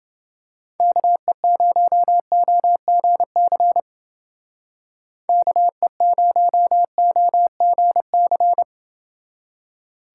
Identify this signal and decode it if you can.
Morse code, heard as a keyed tone, about 29 wpm, 700 Hz. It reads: KE0OGC KE0OGC